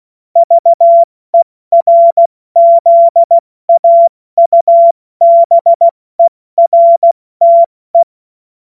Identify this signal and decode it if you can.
Morse code, heard as a keyed tone, about 16 wpm, 680 Hz. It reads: VERZAUBERTE
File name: Morsekurs / Witze / Witz 05. April 2022 70 F80